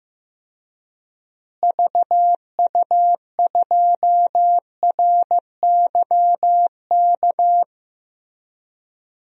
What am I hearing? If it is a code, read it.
Morse code, heard as a keyed tone, about 15 wpm, 700 Hz. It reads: VU2RYK
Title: DXCC-100 Call Signs - 6 Characters - Repeated 3x - 15wpm